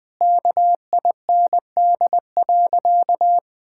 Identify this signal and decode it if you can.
Morse code, heard as a keyed tone, about 20 wpm, 700 Hz. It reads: KIND.
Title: Sentences with 6-Word Phrases - 20wpm